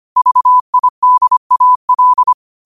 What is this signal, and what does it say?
Morse code, heard as a keyed tone, about 25 wpm, 1000 Hz. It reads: VIDAL
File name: qotdfiles / 25wpm